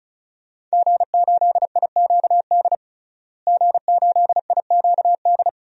Morse code, heard as a keyed tone, about 36 wpm, 700 Hz. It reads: G8IQD G8IQD
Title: DXCC-100 Call Signs - 5 Characters - Repeated 2x - 35wpm